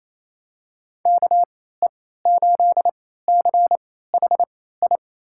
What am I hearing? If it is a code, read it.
Morse code, heard as a keyed tone, about 28 wpm, spaced as for 16 wpm, 700 Hz. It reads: KE8CHI
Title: USA Call Signs - 6 Characters - 3x Character Spacing - 28wpm